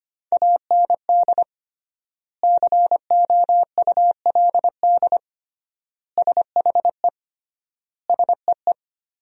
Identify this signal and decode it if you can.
Morse code, heard as a keyed tone, about 25 wpm, 700 Hz. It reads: AND COULD SHE SEE
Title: 3x Word-Spacing - Sentences from Top 100 Words - 25wpm